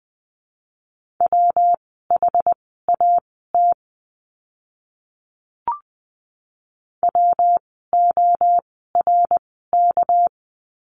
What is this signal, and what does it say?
Morse code, heard as a keyed tone, about 20 wpm, spaced as for 14 wpm, 700 Hz. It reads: WHAT WORK
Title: 244 Lesson 39f - Letter z - Words 20wpm